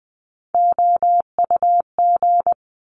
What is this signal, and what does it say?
Morse code, heard as a keyed tone, about 20 wpm, 700 Hz. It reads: OUG
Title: Top Three Letter Combinations in Words 20wpm